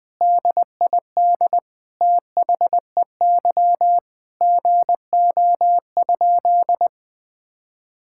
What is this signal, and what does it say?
Morse code, heard as a keyed tone, about 20 wpm, 700 Hz. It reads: DID THEY GO?
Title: Sentences from Top 1600 Words 20wpm